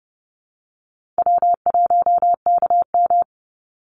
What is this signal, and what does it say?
Morse code, heard as a keyed tone, about 30 wpm, 700 Hz. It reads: W1KM
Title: Call Signs Repeated 3x - 30wpm